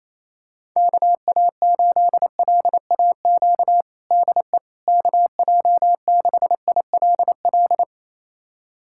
Morse code, heard as a keyed tone, about 28 wpm, 700 Hz. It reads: KA8LAQ DE KJ6ILL
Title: USA Callsign DE Callsign 28wpm